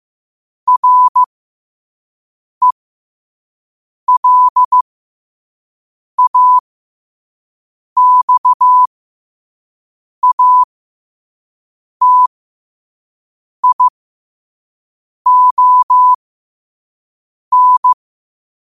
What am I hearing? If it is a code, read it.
Morse code, heard as a keyed tone, about 15 wpm, spaced as for 5 wpm, 1000 Hz. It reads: RELAXATION
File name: qotdfiles / 5wpm